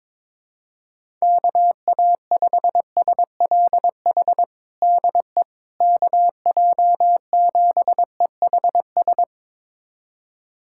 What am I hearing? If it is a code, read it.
Morse code, heard as a keyed tone, about 21 wpm, 700 Hz. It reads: KA5SLH DE KJ7EHS
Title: USA Callsign DE Callsign 22wpm